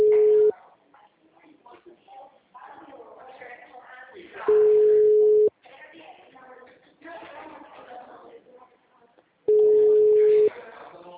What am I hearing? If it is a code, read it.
Ringback tone (Europe, 425 Hz), with speech.
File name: out- -2006- - - .54000